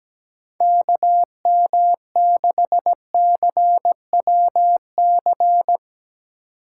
Morse code, heard as a keyed tone, about 17 wpm, 700 Hz. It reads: K M 6 C W C
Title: USA Call Signs - 6 Characters - 6x Character Spacing - 17wpm